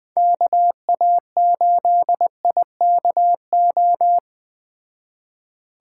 Morse code, heard as a keyed tone, about 21 wpm, 700 Hz. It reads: KA8IKO